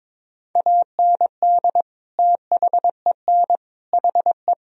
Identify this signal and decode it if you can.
Morse code, heard as a keyed tone, about 22 wpm, 700 Hz. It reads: AND THEN HE
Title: Sentences from Top 1000 Words - Review Part 2 - 22wpm